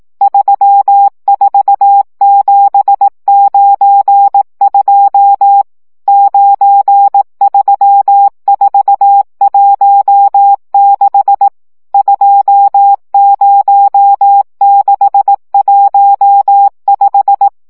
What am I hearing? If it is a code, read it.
Morse code, heard as a keyed tone, about 18 wpm, 800 Hz. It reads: 34792 93416 20615
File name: morze vizsga 2011-3 15